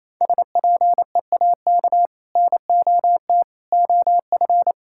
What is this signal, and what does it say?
Morse code, heard as a keyed tone, about 28 wpm, 700 Hz. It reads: SPEAK NOT OF
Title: Sentences from Top 1600 Words 28wpm